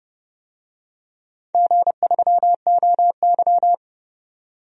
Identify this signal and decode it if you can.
Morse code, heard as a keyed tone, about 29 wpm, 700 Hz. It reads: G3OY
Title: DXCC-100 Call Signs - 4 Characters - 30wpm